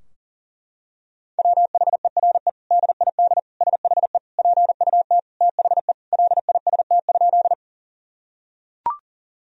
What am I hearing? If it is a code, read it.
Morse code, heard as a keyed tone, about 37 wpm, 700 Hz. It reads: WHERE DID SHE PUT THE LIST?
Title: Sentences from Top 1000 Words - Review Part 2 - 40wpm